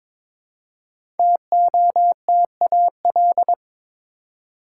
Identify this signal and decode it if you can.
Morse code, heard as a keyed tone, about 22 wpm, 700 Hz. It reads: TOTAL
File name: 5 Letter Words - First 3 Letters Given - 22wpm